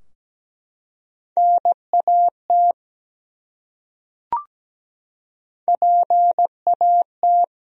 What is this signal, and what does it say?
Morse code, heard as a keyed tone, about 17 wpm, 700 Hz. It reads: NAT PAT
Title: USA Common Names - Max 3 Letters - 17wpm